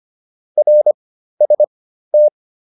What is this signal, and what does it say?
Morse code, heard as a keyed tone, about 25 wpm, spaced as for 13 wpm, 600 Hz. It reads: RST
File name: QSO402 13